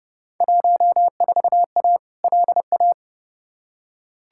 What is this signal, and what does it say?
Morse code, heard as a keyed tone, about 32 wpm, 700 Hz. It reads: 14A LA
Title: Field Day Exchanges Encore - 30wpm